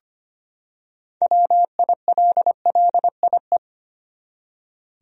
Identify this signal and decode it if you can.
Morse code, heard as a keyed tone, about 25 wpm, 700 Hz. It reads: WILLIE
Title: USA Common Names - 25wpm